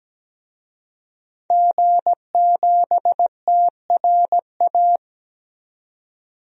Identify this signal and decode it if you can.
Morse code, heard as a keyed tone, about 17 wpm, 700 Hz. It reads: G7TRA
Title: DXCC-100 Call Signs - 5 Characters - 17wpm